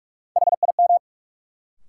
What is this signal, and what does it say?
Morse code, heard as a keyed tone, about 43 wpm, 700 Hz. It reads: HIM
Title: Sentences from Top 100 Words - New Word Drill - 45wpm